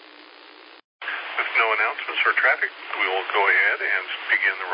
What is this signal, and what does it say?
Emergency Alert System (SAME) tones, heard as harsh buzzing bursts, with speech. It is heard from 1.0 to 4.1 s.